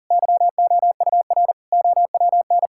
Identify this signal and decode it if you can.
Morse code, heard as a keyed tone, about 40 wpm, 700 Hz. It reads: YOUR OWN
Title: Sentences from Top 1000 Words - Review Part 5 - 40wpm